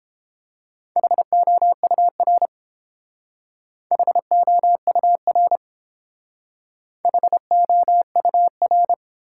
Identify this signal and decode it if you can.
Morse code, heard as a keyed tone, about 29 wpm, 700 Hz. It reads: HOUR HOUR HOUR